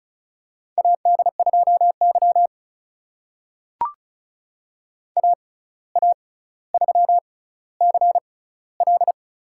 Morse code, heard as a keyed tone, about 36 wpm, spaced as for 17 wpm, 700 Hz. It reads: AD2Y AA3CL
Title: USA Call Signs - 6x Character Spacing - 35wpm